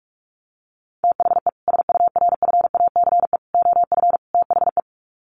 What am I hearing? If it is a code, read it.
Morse code, heard as a keyed tone, about 44 wpm, 700 Hz. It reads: THE SURFACE OF THE
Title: Sets of 4 Words 45wpm